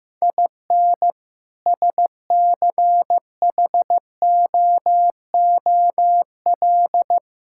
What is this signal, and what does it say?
Morse code, heard as a keyed tone, about 15 wpm, 700 Hz. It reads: IN SCHOOL
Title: Sentences from Top 1000 Words - Review Part 5 - 15wpm